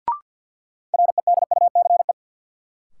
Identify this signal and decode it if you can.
Morse code, heard as a keyed tone, about 53 wpm, 700 Hz. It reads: REDUCE
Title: Sentences from Top 1100 Words - New Word Drill - 50wpm